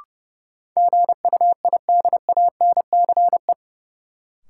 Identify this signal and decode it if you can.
Morse code, heard as a keyed tone, about 30 wpm, 700 Hz. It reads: GUIDANCE
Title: Sentences from Top 1600 Words - New Word Drill - 30wpm